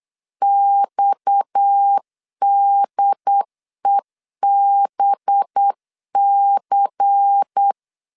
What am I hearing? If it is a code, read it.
Morse code, heard as a keyed tone, about 8 wpm, 800 Hz. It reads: XDEBC
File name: LEZIONI / 7 GF9Z / 7 GF9 8wpm 3545